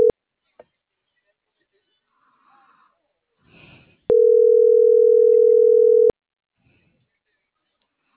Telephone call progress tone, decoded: ringback tone

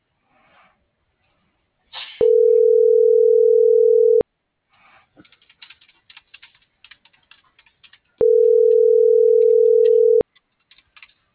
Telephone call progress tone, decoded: ringback tone

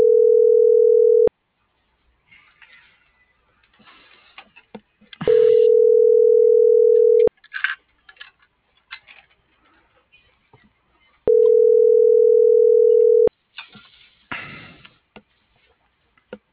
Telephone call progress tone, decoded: ringback tone

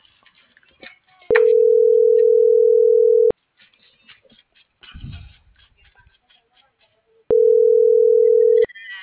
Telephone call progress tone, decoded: ringback tone

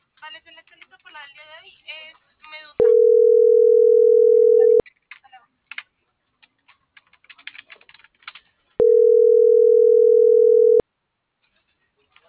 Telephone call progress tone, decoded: ringback tone

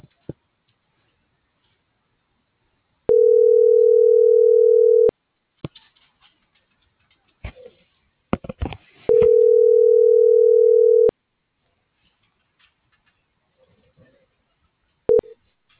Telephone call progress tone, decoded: ringback tone

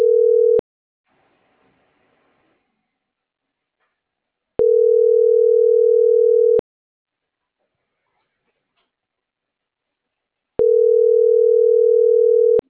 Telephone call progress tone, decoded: ringback tone

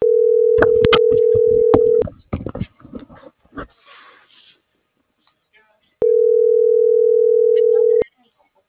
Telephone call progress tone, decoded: ringback tone

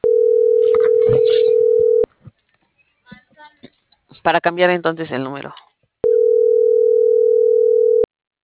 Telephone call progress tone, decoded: ringback tone

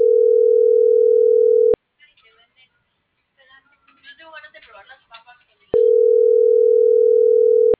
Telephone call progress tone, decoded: ringback tone

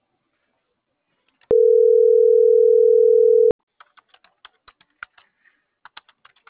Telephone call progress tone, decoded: ringback tone

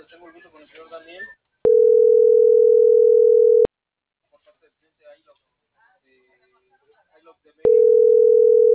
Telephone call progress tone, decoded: ringback tone

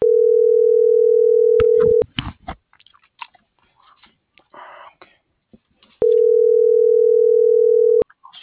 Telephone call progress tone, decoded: ringback tone

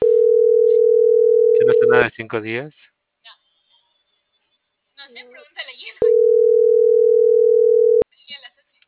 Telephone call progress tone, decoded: ringback tone

